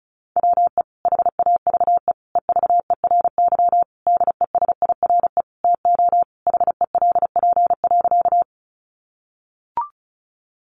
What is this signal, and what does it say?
Morse code, heard as a keyed tone, about 34 wpm, 700 Hz. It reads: WE HAVE EVERY DESIRE TO HELP.
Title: Sentences from Top 1100 Words 35wpm